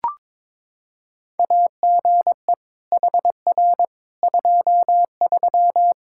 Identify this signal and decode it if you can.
Morse code, heard as a keyed tone, about 22 wpm, 700 Hz. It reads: AGE HR 23
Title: K1USN SST QSO Practice 22wpm